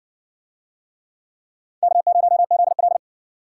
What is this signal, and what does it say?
Morse code, heard as a keyed tone, about 55 wpm, 700 Hz. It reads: K0ZL